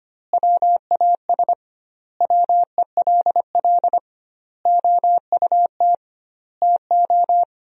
Morse code, heard as a keyed tone, about 25 wpm, 700 Hz. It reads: WAS WELL OUT TO
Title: 2x Word-Spacing - Sentences from Top 400 Words - 25wpm